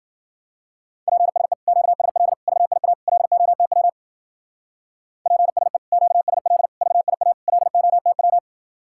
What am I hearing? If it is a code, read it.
Morse code, heard as a keyed tone, about 50 wpm, 700 Hz. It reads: PSE QSL VIA LOTW PSE QSL VIA LOTW
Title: Speed-Racing Common QSO Phrases - 40wpm